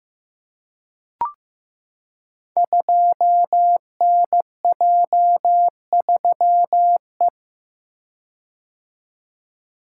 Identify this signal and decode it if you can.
Morse code, heard as a keyed tone, about 15 wpm, 700 Hz. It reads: 2NJ3E
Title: Code Groups - 5 Letter:Numbers - 15wpm